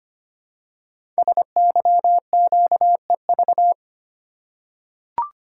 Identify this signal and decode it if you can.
Morse code, heard as a keyed tone, about 25 wpm, 700 Hz. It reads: SYQEV
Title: Code Groups - 5 Letters - 25wpm